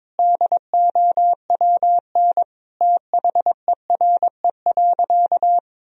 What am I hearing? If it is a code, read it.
Morse code, heard as a keyed tone, about 21 wpm, 700 Hz. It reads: DOWN THERE.
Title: Sentences from Top 1000 Words - Review Part 4 - 22wpm